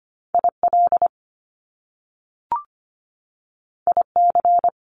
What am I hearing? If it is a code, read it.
Morse code, heard as a keyed tone, about 25 wpm, 700 Hz. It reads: IL IC
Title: Top Two Letter Combinations in Words 25wpm